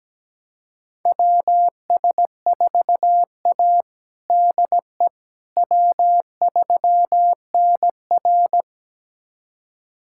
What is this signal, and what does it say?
Morse code, heard as a keyed tone, about 17 wpm, 700 Hz. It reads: WS4A DE W3NR
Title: USA Callsign DE Callsign - 4 Characters - 17wpm